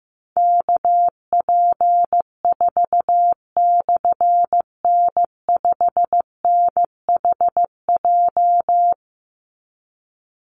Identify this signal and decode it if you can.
Morse code, heard as a keyed tone, about 15 wpm, 700 Hz. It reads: KP4/N5NHJ
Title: Call Signs Repeated 3x - 15wpm